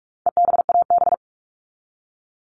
Morse code, heard as a keyed tone, about 44 wpm, 700 Hz. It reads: EBAB
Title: All Cut Numbers - 4-Digit 45wpm